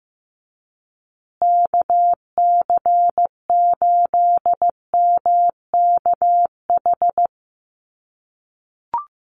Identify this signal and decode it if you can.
Morse code, heard as a keyed tone, about 15 wpm, 700 Hz. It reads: KC8MKH
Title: USA Call Signs Repeated 2x 15wpm